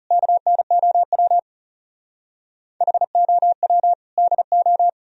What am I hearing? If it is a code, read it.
Morse code, heard as a keyed tone, about 36 wpm, 700 Hz. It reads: KNOW HOW DO
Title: Speed-Racing - Sets of 4 Words 30wpm